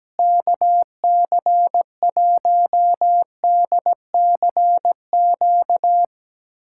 Morse code, heard as a keyed tone, about 17 wpm, 700 Hz. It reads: KC1DCQ
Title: USA Callsign DE Callsign 17wpm